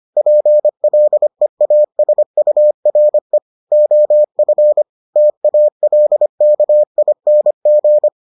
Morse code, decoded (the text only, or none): PLEASURE OF TALKING